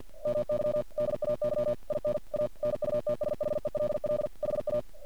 Morse code, heard as a keyed tone, about 50 wpm, 600 Hz. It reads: MY DAY IN A NUTSHELL HA
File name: archived / CRSnet